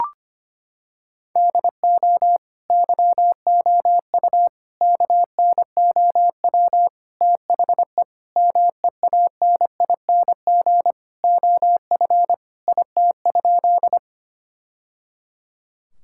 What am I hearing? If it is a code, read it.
Morse code, heard as a keyed tone, about 25 wpm, 700 Hz. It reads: DO YOU KNOW THE MEANING OF IT?
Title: Sentences from Top 1000 Words - Review Part 6 - 25wpm